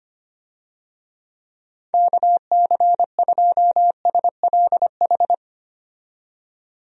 Morse code, heard as a keyed tone, about 25 wpm, 700 Hz. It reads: KC2SLH
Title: USA Call Signs - 6 Characters - 25wpm